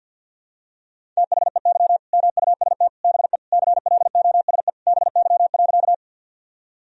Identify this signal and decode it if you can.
Morse code, heard as a keyed tone, about 52 wpm, 700 Hz. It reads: THEY MUST BE CLOSE BY.
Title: Sentences from Top 1000 Words - Review Part 1 - 50wpm